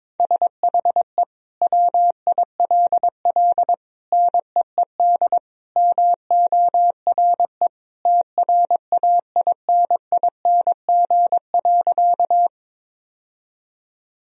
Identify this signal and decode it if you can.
Morse code, heard as a keyed tone, about 22 wpm, 700 Hz. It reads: SHE WILL NEED MORE TRAINING.
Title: Sentences from Top 1000 Words - Review Part 4 - 22wpm